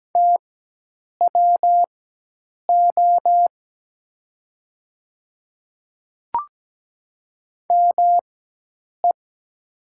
Morse code, heard as a keyed tone, about 17 wpm, spaced as for 8 wpm, 700 Hz. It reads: TWO ME
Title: 4x Character Spacing - Top 300 Words - 17wpm